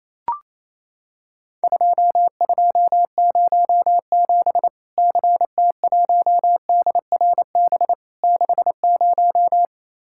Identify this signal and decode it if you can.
Morse code, heard as a keyed tone, about 28 wpm, 700 Hz. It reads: 2207 CT1DRB 60
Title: Worked All Europe DX Contest - QTC Exchange - 28wpm